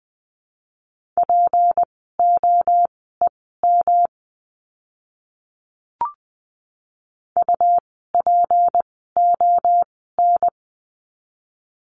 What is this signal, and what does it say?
Morse code, heard as a keyed tone, about 20 wpm, spaced as for 14 wpm, 700 Hz. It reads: POEM UPON
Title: 153 Lesson 25f - Letter p - Words 20wpm